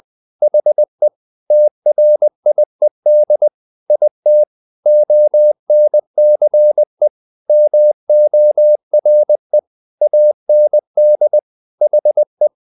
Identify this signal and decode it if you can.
Morse code, heard as a keyed tone, about 20 wpm, 600 Hz. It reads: HE TRIED IT ONCE MORE AND HE
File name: the old man and the sea part 2 20wpm